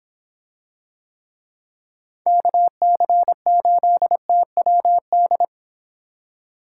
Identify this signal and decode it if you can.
Morse code, heard as a keyed tone, about 24 wpm, 700 Hz. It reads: KC8TWD